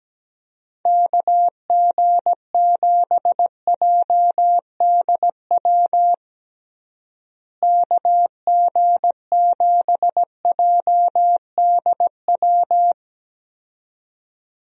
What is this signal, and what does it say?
Morse code, heard as a keyed tone, about 17 wpm, 700 Hz. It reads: KG7JDW KG7JDW
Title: USA Call Signs - 6 Characters - Repeated 2x - 17wpm